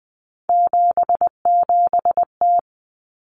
Morse code, heard as a keyed tone, about 20 wpm, 700 Hz. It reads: 77T
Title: Common Cut Numbers - 3-Digit 20wpm